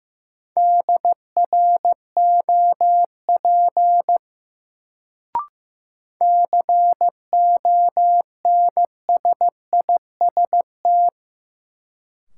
Morse code, heard as a keyed tone, about 15 wpm, 700 Hz. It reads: DROP CONSIST
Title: Sentences from Top 1400 Words - New Word Drill - 15wpm